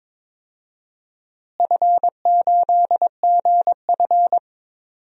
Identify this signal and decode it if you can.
Morse code, heard as a keyed tone, about 22 wpm, 700 Hz. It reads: F8GF
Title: DXCC-100 Call Signs - 4 Characters - Repeated 3x - 22wpm